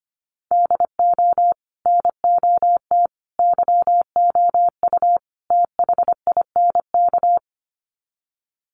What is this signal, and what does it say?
Morse code, heard as a keyed tone, about 25 wpm, 700 Hz. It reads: DO NOT YOU THINK
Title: Sets of 4 Words 25wpm